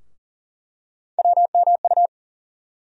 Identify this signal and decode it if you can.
Morse code, heard as a keyed tone, about 40 wpm, 700 Hz. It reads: WMU
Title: USA Call Sign Postfixes - 3 Letters 40wpm